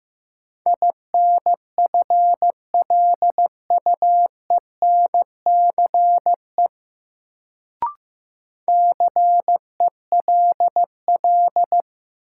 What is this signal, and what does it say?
Morse code, heard as a keyed tone, about 15 wpm, 700 Hz. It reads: INFLUENCE CELL
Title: Sentences from Top 1100 Words - New Word Drill - 15wpm